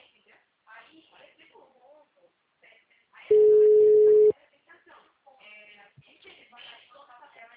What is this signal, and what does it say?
Ringback tone (Europe, 425 Hz), with speech.